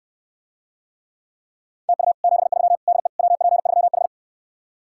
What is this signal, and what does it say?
Morse code, heard as a keyed tone, about 69 wpm, 700 Hz. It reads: TU 73 DE WP3H